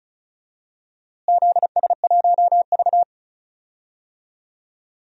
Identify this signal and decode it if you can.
Morse code, heard as a keyed tone, about 34 wpm, 700 Hz. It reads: ZS1V
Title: DXCC-100 Call Signs - 4 Characters - Repeated 2x - 35wpm